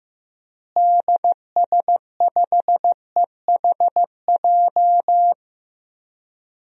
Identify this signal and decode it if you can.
Morse code, heard as a keyed tone, about 16 wpm, 700 Hz. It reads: DS5EHJ
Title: DXCC-100 Call Signs - 6 Characters - Repeated 2x - 15wpm